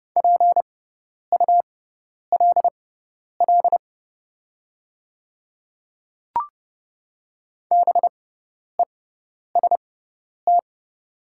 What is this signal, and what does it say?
Morse code, heard as a keyed tone, about 30 wpm, spaced as for 10 wpm, 700 Hz. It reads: PULL BEST
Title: 6x Character Spacing - Top 500 Words - 30wpm